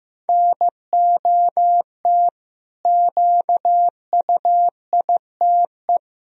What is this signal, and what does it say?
Morse code, heard as a keyed tone, about 15 wpm, 700 Hz. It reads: NOT QUITE
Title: Sentences from Top 1000 Words - Review Part 1 - 15wpm